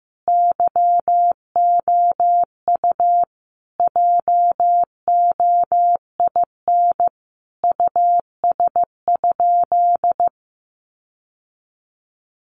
Morse code, heard as a keyed tone, about 15 wpm, 700 Hz. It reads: YOU JOIN US?
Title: Sentences from Top 1400 Words 15wpm